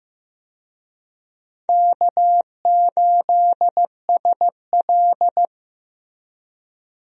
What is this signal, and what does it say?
Morse code, heard as a keyed tone, about 15 wpm, 700 Hz. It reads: K8SL
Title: USA Call Signs - 4 Characters - Repeated 2x - 15wpm